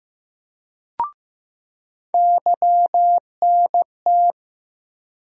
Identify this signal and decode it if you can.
Morse code, heard as a keyed tone, about 15 wpm, 700 Hz. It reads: YNT
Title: USA Call Sign Postfixes - 3 Letters 15wpm